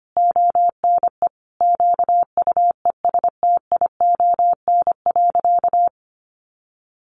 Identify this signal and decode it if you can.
Morse code, heard as a keyed tone, about 25 wpm, 700 Hz. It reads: ONE QUESTION.
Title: Sentences from Top 400 Words 25wpm